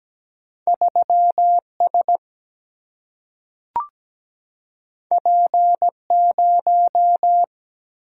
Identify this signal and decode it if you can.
Morse code, heard as a keyed tone, about 17 wpm, 700 Hz. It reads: 3S P0
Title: Single-Letter Single-Number 17wpm